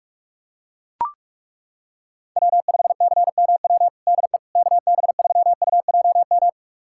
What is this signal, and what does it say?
Morse code, heard as a keyed tone, about 41 wpm, 700 Hz. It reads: W5CMW DE KB3UJM
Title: USA Callsign DE Callsign 45wpm